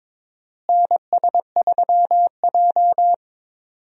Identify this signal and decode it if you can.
Morse code, heard as a keyed tone, about 22 wpm, 700 Hz. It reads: NS3J
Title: USA Call Signs - 4 Characters - Repeated 3x - 22wpm